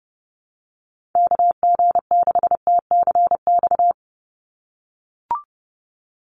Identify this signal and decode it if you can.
Morse code, heard as a keyed tone, about 30 wpm, 700 Hz. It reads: KG6TCX
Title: USA Call Signs 30wpm